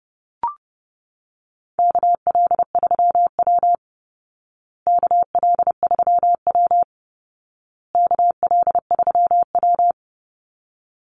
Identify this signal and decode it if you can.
Morse code, heard as a keyed tone, about 30 wpm, 700 Hz. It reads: KL3W KL3W KL3W
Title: Call Signs Repeated 2x - 30wpm